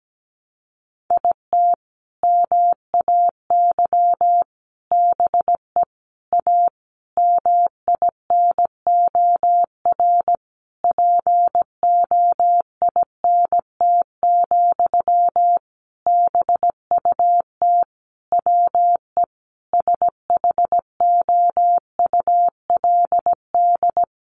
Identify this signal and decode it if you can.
Morse code, heard as a keyed tone, about 17 wpm, 700 Hz. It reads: IT MAY BE A MINOR POINT, BUT WE SHOULD
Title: Sentences from Top 1500 Words 17wpm